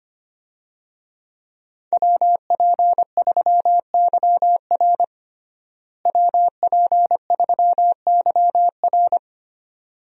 Morse code, heard as a keyed tone, about 25 wpm, 700 Hz. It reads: WP3YR WP3YR
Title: USA Call Signs - 5 Characters - Repeated 2x - 25wpm